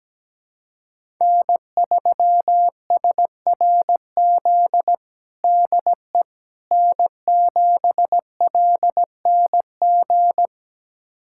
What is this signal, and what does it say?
Morse code, heard as a keyed tone, about 17 wpm, 700 Hz. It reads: N3SRZ DE N7LNG
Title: USA Callsign DE Callsign - 5 Characters - 17wpm